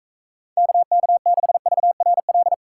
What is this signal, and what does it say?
Morse code, heard as a keyed tone, about 43 wpm, 700 Hz. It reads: KK6VRL